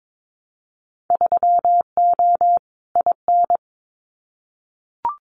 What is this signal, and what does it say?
Morse code, heard as a keyed tone, about 22 wpm, 700 Hz. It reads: 3O IN
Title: Winter Field Day Exchanges Encore 22wpm